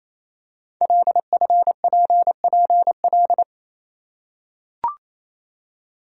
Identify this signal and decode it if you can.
Morse code, heard as a keyed tone, about 29 wpm, 700 Hz. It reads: LFPPL
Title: Code Groups - 5 Letters - 28wpm